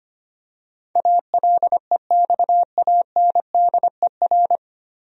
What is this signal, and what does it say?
Morse code, heard as a keyed tone, about 25 wpm, 700 Hz. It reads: ALEXANDER
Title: USA Common Names - 25wpm